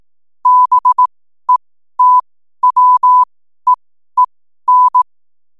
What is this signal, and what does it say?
Morse code, heard as a keyed tone, about 18 wpm, spaced as for 12 wpm, 1000 Hz. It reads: BETWEEN
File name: n7net / 12wpm